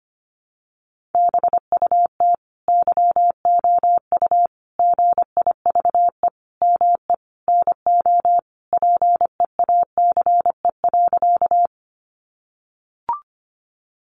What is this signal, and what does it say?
Morse code, heard as a keyed tone, about 25 wpm, 700 Hz. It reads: BUT YOU GIVE ME NO PEACE.
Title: Sentences from Top 1500 Words 25wpm